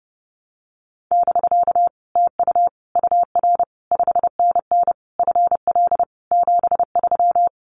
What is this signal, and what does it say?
Morse code, heard as a keyed tone, about 30 wpm, 700 Hz. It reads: <BK> TU UR 5NN FL 73
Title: Common QSO Phrases Encore 30wpm